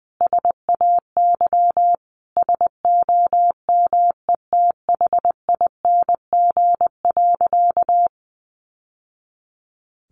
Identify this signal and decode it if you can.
Morse code, heard as a keyed tone, about 20 wpm, 700 Hz. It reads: SAY SOMETHING.
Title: Sentences from Top 600 Words 20wpm